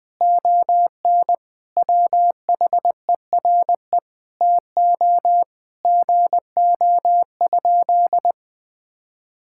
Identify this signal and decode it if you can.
Morse code, heard as a keyed tone, about 21 wpm, 700 Hz. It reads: ON WHERE TO GO?
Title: Sentences from Top 1600 Words 20wpm